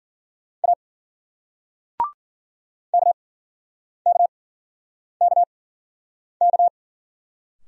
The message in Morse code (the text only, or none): A KKKK